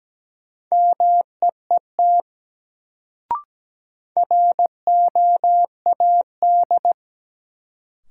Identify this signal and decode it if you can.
Morse code, heard as a keyed tone, about 17 wpm, 700 Hz. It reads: MEET ROAD
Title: Sets of 5 Words - New Word Drill - 17wpm